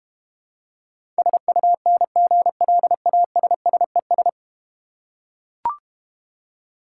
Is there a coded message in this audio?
Yes: Morse code from 1.2 to 4.3 s, heard as a keyed tone, about 32 wpm, 700 Hz. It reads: SUNGLASSES